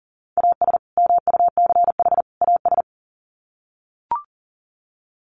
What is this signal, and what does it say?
Morse code, heard as a keyed tone, about 40 wpm, 700 Hz. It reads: AS MUCH AS
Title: Sets of 3 Words 40wpm